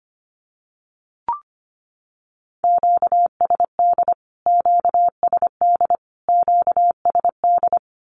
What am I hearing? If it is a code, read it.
Morse code, heard as a keyed tone, about 25 wpm, 700 Hz. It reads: QSD QSD QSD
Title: Q-Codes Repeated 3x 25wpm